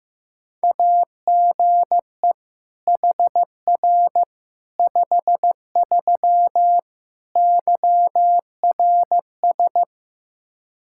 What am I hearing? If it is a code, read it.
Morse code, heard as a keyed tone, about 15 wpm, 700 Hz. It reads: AGE HR 53 YRS
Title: Practice QSO - Exchange 3 - More Info - 15wpm